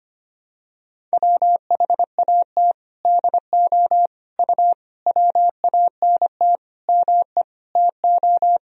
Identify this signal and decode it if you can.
Morse code, heard as a keyed tone, about 25 wpm, 700 Hz. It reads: WHAT DO U WANT ME TO
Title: Sentences from Top 200 Words and Abbreviations 25wpm